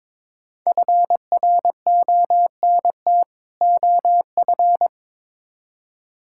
Morse code, heard as a keyed tone, about 21 wpm, 700 Hz. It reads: FRONT OF
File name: Build A Phrase - 2-Word Phrases - 22wpm